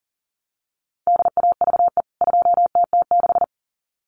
Morse code, heard as a keyed tone, about 40 wpm, 700 Hz. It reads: DAVE 2TT6
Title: CWT Exchange 40wpm